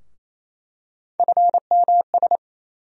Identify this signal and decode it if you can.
Morse code, heard as a keyed tone, about 28 wpm, 700 Hz. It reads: FMS
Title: USA Call Sign Postfixes - 3 Letters 28wpm